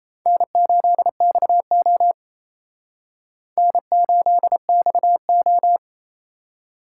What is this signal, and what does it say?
Morse code, heard as a keyed tone, about 38 wpm, 700 Hz. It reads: N8XO N8XO